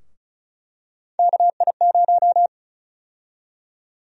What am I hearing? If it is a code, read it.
Morse code, heard as a keyed tone, about 36 wpm, 700 Hz. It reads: KI0
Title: USA Call Sign Prefixes - 35wpm